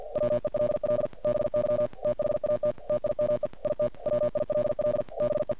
Morse code, heard as a keyed tone, about 50 wpm, 600 Hz, with noise. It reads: WELL BY THAT TIME IT WILL BE
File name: archived / CRSnet